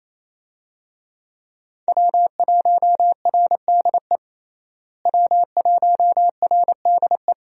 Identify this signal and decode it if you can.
Morse code, heard as a keyed tone, about 28 wpm, 700 Hz. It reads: W1RDE W1RDE
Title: USA Call Signs - 5 Characters - Repeated 2x - 28wpm